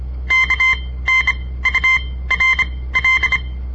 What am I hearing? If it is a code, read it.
Morse code, heard as a keyed tone, about 26 wpm, spaced as for 17 wpm, 2100 Hz. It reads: KNURL
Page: On the air Morse code practice from the Southern Montana Amatuer Radio Association; all code, no read back